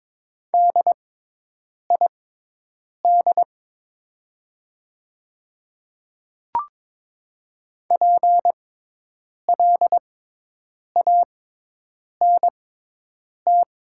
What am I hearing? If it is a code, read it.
Morse code, heard as a keyed tone, about 22 wpm, spaced as for 8 wpm, 700 Hz. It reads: DID PLANT
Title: 6x Character Spacing - Top 300 Words - 22wpm